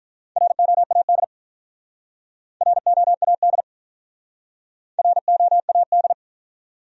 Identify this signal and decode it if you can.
Morse code, heard as a keyed tone, about 46 wpm, 700 Hz. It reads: ROAD ROAD ROAD